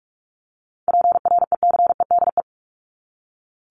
Morse code, heard as a keyed tone, about 44 wpm, 700 Hz. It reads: PRECEDE
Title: ING Suffix - Top 100 Words with Base Words 45wpm